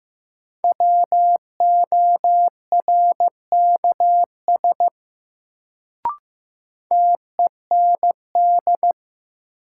Morse code, heard as a keyed tone, about 15 wpm, 700 Hz. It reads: WORKS TEND
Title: Sentences from Top 1000 Words - Review Part 5 - New Word Drill - 15wpm